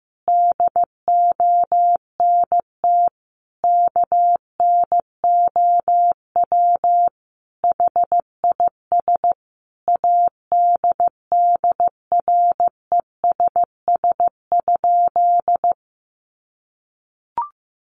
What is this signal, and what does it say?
Morse code, heard as a keyed tone, about 14 wpm, 700 Hz. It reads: DONT KNOW HIS ADDRESS?
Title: Sentences from Top 1000 Words 15wpm